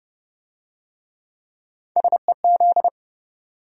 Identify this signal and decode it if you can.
Morse code, heard as a keyed tone, about 27 wpm, 700 Hz. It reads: SEZ